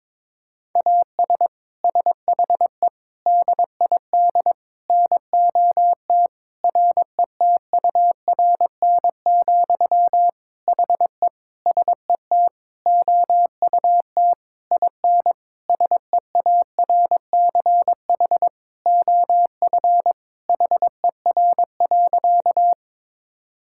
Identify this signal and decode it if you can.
Morse code, heard as a keyed tone, about 22 wpm, 700 Hz. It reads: AS SHE DID NOT RETURN, HE SET OUT IN SEARCH OF HER.
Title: Sentences from Top 1000 Words - Review Part 5 - 22wpm